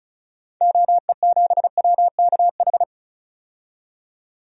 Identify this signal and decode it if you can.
Morse code, heard as a keyed tone, about 36 wpm, 700 Hz. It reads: OE7WKH
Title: DXCC-100 Call Signs - 6 Characters - 35wpm